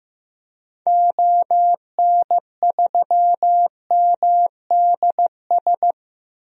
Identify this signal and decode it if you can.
Morse code, heard as a keyed tone, about 15 wpm, 700 Hz. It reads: ON3MDS
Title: DXCC-100 Call Signs - 6 Characters - Repeated 2x - 15wpm